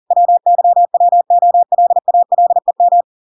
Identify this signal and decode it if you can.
Morse code, heard as a keyed tone, about 40 wpm, 700 Hz. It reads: WYWOLALEM